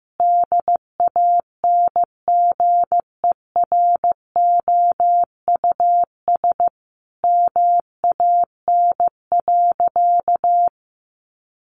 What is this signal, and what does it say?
Morse code, heard as a keyed tone, about 15 wpm, 700 Hz. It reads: DANGEROUS MAN.
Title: Sentences from Top 1200 Words 15wpm